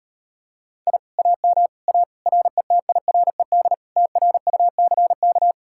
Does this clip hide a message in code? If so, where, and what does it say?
Yes: Morse code from 0.9 to 5.5 s, heard as a keyed tone, about 33 wpm, 700 Hz. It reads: I AM A RETIRED TRUCK